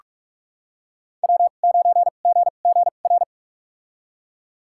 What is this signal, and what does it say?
Morse code, heard as a keyed tone, about 43 wpm, spaced as for 32 wpm, 700 Hz. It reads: W9GGR
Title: USA Call Signs - 2x Character Spacing - 45wpm